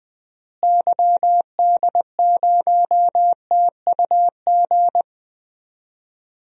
Morse code, heard as a keyed tone, about 21 wpm, 700 Hz. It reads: YD0TUG
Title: DXCC-100 Call Signs - 6 Characters - Repeated 2x - 20wpm